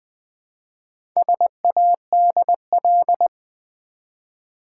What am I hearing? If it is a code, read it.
Morse code, heard as a keyed tone, about 20 wpm, 700 Hz. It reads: SADL
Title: Build A Word - 1 Letter At A Time - 5-Letter Words - 20wpm